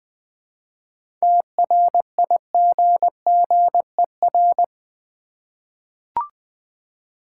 Morse code, heard as a keyed tone, about 20 wpm, 700 Hz. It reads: TRIGGER
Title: 7 Letter Words - First 3 Letters Given - 20wpm